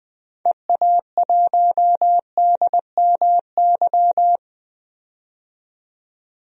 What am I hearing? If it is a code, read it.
Morse code, heard as a keyed tone, about 21 wpm, 700 Hz. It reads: EA1DMY
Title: DXCC-100 Call Signs - 6 Characters - 20wpm